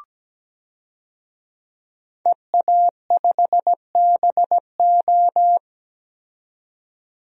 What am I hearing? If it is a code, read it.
Morse code, heard as a keyed tone, about 17 wpm, 700 Hz. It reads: EA5BO
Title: DXCC-100 Call Signs - 5 Characters - 17wpm